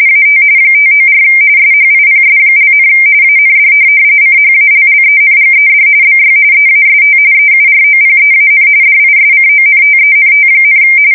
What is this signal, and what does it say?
RTTY, about 45.45 baud. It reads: AT 594 SHIPS), AND CHALLENGED BY EAST EUROPEAN DISCONTENT,
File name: fsk170 / USS-New-Jersey